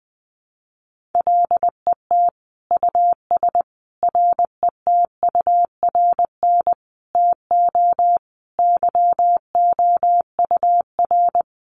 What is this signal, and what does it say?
Morse code, heard as a keyed tone, about 20 wpm, 700 Hz. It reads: LET US RETURN TO YOUR
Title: Sentences from Top 800 Words 20wpm